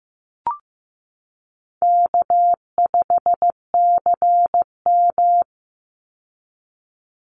Morse code, heard as a keyed tone, about 14 wpm, 700 Hz. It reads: K5CM
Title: Easy Call Signs Repeated 2x - 15wpm